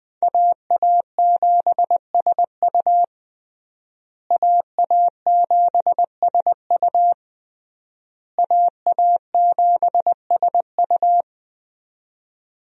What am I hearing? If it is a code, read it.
Morse code, heard as a keyed tone, about 20 wpm, 700 Hz. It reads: AA7SU AA7SU AA7SU
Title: USA Call Signs - 5 Characters - Repeated 3x - 20wpm